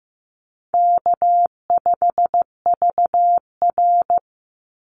Morse code, heard as a keyed tone, about 15 wpm, 700 Hz. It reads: K5VR
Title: Call Signs Repeated 2x - 15wpm